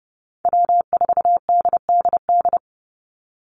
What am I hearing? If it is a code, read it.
Morse code, heard as a keyed tone, about 30 wpm, 700 Hz. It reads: W4DDD
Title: Call Signs Repeated 2x - 30wpm